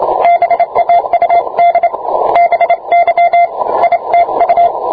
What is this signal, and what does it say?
Morse code, heard as a keyed tone, about 27 wpm, 690 Hz, with radio noise. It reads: BAUD BY ITU